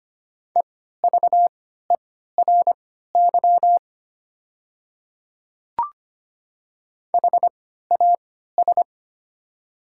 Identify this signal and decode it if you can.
Morse code, heard as a keyed tone, about 25 wpm, spaced as for 14 wpm, 700 Hz. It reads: EVERY HAS
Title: 3x Character Spacing - Top 200 Words - 25wpm